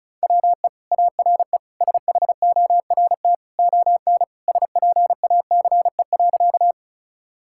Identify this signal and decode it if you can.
Morse code, heard as a keyed tone, about 36 wpm, 690 Hz. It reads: WE ARE SHORT ON SPACE.
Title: Sentences from Top 1000 Words - Review Part 5 - 35wpm